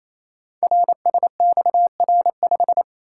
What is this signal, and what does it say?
Morse code, heard as a keyed tone, about 27 wpm, 700 Hz. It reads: RSXR5
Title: Code Groups - 5 Letter:Numbers - 28wpm